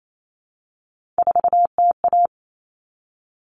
Morse code, heard as a keyed tone, about 28 wpm, 700 Hz. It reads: 4TA
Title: All Cut Numbers - 3-Digit 28wpm